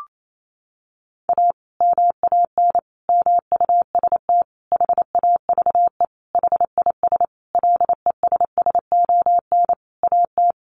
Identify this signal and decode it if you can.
Morse code, heard as a keyed tone, about 28 wpm, 700 Hz. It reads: A MAN MUST HAVE HIS LESSON AT
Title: Sentences from Top 1200 Words 28wpm